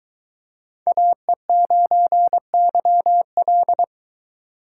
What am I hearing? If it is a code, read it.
Morse code, heard as a keyed tone, about 22 wpm, 700 Hz. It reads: AE9YL